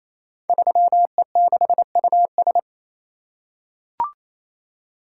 Morse code, heard as a keyed tone, about 27 wpm, 700 Hz. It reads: 3E6US
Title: Code Groups - 5 Letter:Numbers - 28wpm